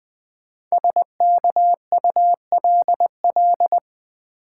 Morse code, heard as a keyed tone, about 20 wpm, 700 Hz. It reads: SKULL
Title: 5 Letter Words - First 3 Letters Given - 20wpm